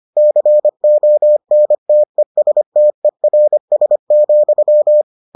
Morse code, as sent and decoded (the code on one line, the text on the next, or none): -.-. --- -. - . ... - . .-. ... --..--
CONTESTERS,